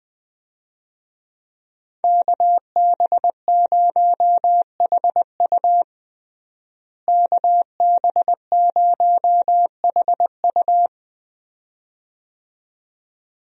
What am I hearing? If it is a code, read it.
Morse code, heard as a keyed tone, about 20 wpm, 700 Hz. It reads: KB0HU KB0HU
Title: USA Call Signs - 5 Characters - Repeated 2x - 20wpm